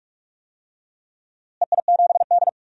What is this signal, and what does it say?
Morse code, heard as a keyed tone, about 46 wpm, 700 Hz. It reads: EI7D
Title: DXCC-100 Call Signs - 4 Characters - Repeated 2x - 45wpm